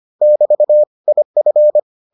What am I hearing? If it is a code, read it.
Morse code, heard as a keyed tone, about 25 wpm, 600 Hz. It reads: <BT> IF